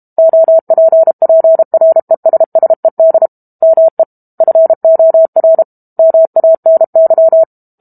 Morse code, heard as a keyed tone, about 32 wpm, 650 Hz. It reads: OPPRESSED ME FOR MANY
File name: Book Two / 06-Chapter 6 - 32 WPM